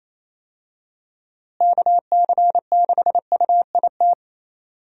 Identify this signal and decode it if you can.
Morse code, heard as a keyed tone, about 28 wpm, 700 Hz. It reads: KC6UIT
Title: USA Call Signs - 6 Characters - Repeated 2x - 28wpm